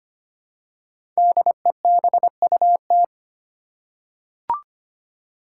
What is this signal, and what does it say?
Morse code, heard as a keyed tone, about 25 wpm, 700 Hz. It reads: DEBUT
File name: 5 Letter Words - First 3 Letters Given - 25wpm